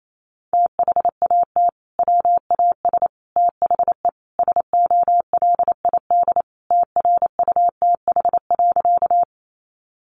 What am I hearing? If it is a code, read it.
Morse code, heard as a keyed tone, about 28 wpm, 700 Hz. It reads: THAT WAS THE SOLID TRUTH.
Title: Sentences from Top 1500 Words 28wpm